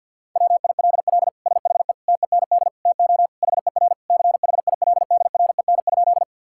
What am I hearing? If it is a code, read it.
Morse code, heard as a keyed tone, about 50 wpm, 700 Hz. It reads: WILL SHE TEND TO HER CHILDREN?
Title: Sentences from Top 1000 Words - Review Part 5 - 50wpm